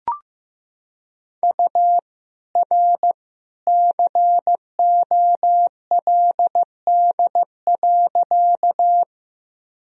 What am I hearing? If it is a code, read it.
Morse code, heard as a keyed tone, about 15 wpm, 700 Hz. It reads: U R COLD.
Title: Sentences from Top 400 Words and Abbreviations 15wpm